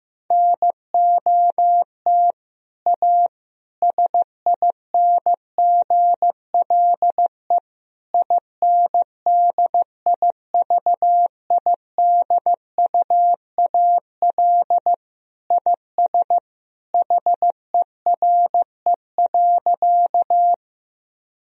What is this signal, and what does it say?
Morse code, heard as a keyed tone, about 15 wpm, 700 Hz. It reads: NOT A SINGLE INDIVIDUAL IS HERE.
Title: Sentences from Top 1000 Words - Review Part 4 - 15wpm